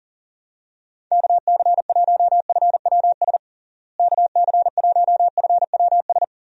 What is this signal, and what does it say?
Morse code, heard as a keyed tone, about 37 wpm, 700 Hz. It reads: KC1FWS KC1FWS
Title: USA Call Signs - 6 Characters - Repeated 2x - 40wpm